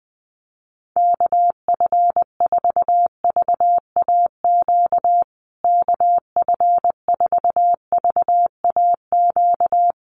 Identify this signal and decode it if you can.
Morse code, heard as a keyed tone, about 20 wpm, 700 Hz. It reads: KF4VAQ KF4VAQ
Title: USA Call Signs Repeated 3x 20wpm